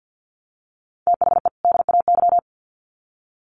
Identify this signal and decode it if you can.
Morse code, heard as a keyed tone, about 50 wpm, 700 Hz. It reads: THE DAY
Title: Sets of 2 Words 50wpm